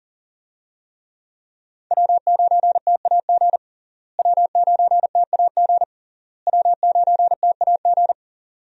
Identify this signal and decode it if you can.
Morse code, heard as a keyed tone, about 40 wpm, 700 Hz. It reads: W9TAG W9TAG W9TAG
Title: USA Call Signs - 5 Characters - Repeated 3x - 40wpm